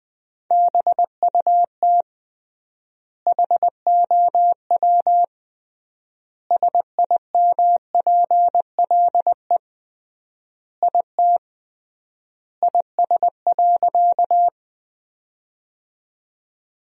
Morse code, heard as a keyed tone, about 20 wpm, 700 Hz. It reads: BUT HOW SIMPLE IT IS.
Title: 3x Word-Spacing - Sentences from Top 400 Words - 20wpm